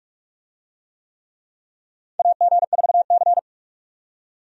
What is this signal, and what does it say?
Morse code, heard as a keyed tone, about 46 wpm, 700 Hz. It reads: AG4C
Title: USA Call Signs - 4 Characters - 45wpm